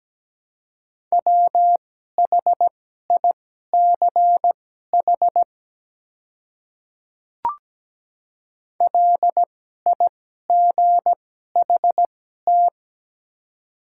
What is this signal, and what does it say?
Morse code, heard as a keyed tone, about 17 wpm, spaced as for 12 wpm, 700 Hz. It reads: WHICH LIGHT
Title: 2x Character Spacing - Top 200 Words - 17wpm